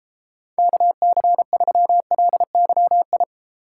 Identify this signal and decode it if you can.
Morse code, heard as a keyed tone, about 33 wpm, 700 Hz. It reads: KC3LYI